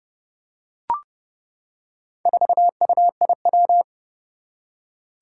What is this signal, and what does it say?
Morse code, heard as a keyed tone, about 30 wpm, 700 Hz. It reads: 4UIW
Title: Four Challenging Letter-Number Combinations 30wpm